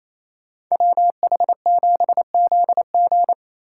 Morse code, heard as a keyed tone, about 28 wpm, 700 Hz. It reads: WH7ZG
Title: Speed Racing - USA 2x2 Call Signs 28wpm